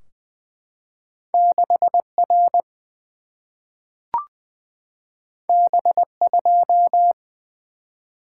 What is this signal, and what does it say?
Morse code, heard as a keyed tone, about 20 wpm, 700 Hz. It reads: 6 R B 2
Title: Single-Letter Single-Number 20wpm